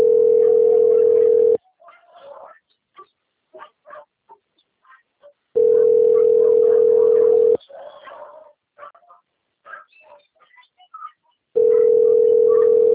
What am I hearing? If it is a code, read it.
Ringback tone (North America), 440 + 480 Hz.
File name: out- - - - - .17726